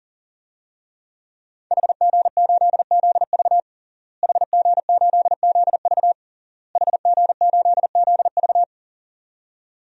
Morse code, heard as a keyed tone, about 40 wpm, 700 Hz. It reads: HG8ZV HG8ZV HG8ZV
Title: DXCC-100 Call Signs - 5 Characters - Repeated 3x - 40wpm